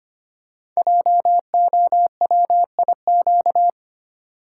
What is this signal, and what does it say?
Morse code, heard as a keyed tone, about 25 wpm, 700 Hz. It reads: JOWIQ
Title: Code Groups - 5 Letters - 25wpm